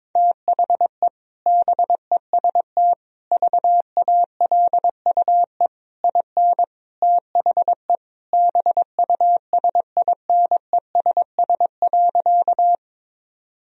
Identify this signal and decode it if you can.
Morse code, heard as a keyed tone, about 22 wpm, 700 Hz. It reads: THE BEST VALUE IN THE BUSINESS.
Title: Sentences from Top 1000 Words - Review Part 3 - 22wpm